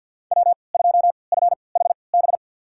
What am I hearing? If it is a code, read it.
Morse code, heard as a keyed tone, about 46 wpm, spaced as for 27 wpm, 690 Hz. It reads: W2FHB
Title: USA Call Signs - 5 Characters - 3x Character Spacing - 50wpm